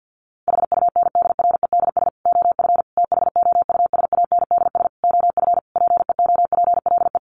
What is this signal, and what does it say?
Morse code, heard as a keyed tone, about 50 wpm, 700 Hz. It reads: HUNDREDS OF THOUSANDS OF PEOPLE
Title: Sets of 5 Words Encore2 50wpm